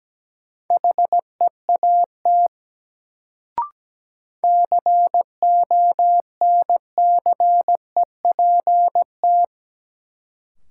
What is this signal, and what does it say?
Morse code, heard as a keyed tone, about 17 wpm, 700 Hz. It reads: HEAT CONCEPT
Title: Sentences from Top 1100 Words - New Word Drill Alt - 17wpm